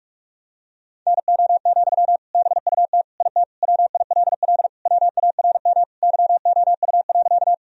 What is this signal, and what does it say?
Morse code, heard as a keyed tone, about 42 wpm, 700 Hz. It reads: NO, BUT IT WILL WARM YOU.
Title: Sentences from Top 1000 Words - Review Part 2 - 45wpm